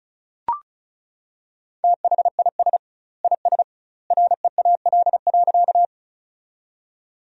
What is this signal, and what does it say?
Morse code, heard as a keyed tone, about 34 wpm, 700 Hz. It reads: THIS IS REAL.
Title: 2x Word-Spacing - Sentences from Top 300 Words - 35wpm